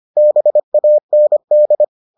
Morse code, heard as a keyed tone, about 25 wpm, 600 Hz. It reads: BAND